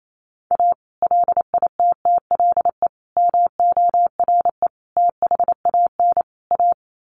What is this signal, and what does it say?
Morse code, heard as a keyed tone, about 28 wpm, 700 Hz. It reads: A LITTLE MORE THAN A
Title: Sets of 5 Words Encore2 28wpm